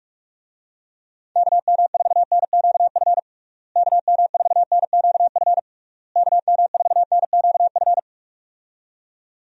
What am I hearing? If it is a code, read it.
Morse code, heard as a keyed tone, about 46 wpm, 700 Hz. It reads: KM4NQF KM4NQF KM4NQF
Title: USA Call Signs - 6 Characters - Repeated 3x - 45wpm